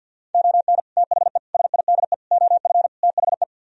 Morse code, heard as a keyed tone, about 65 wpm, spaced as for 57 wpm, 700 Hz. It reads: ON THE SIDE OF THE
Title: Speed-Racing - Sets of 5 Words 50wpm